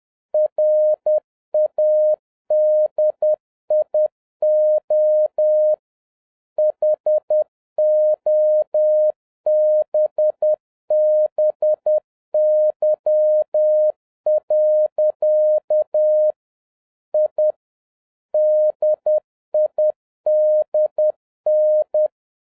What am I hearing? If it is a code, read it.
Morse code, heard as a keyed tone, about 10 wpm, 610 Hz. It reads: RADIO HOBBY. I DIDN